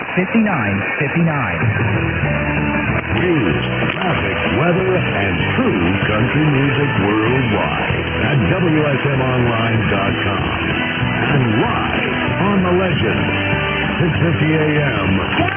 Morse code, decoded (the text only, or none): none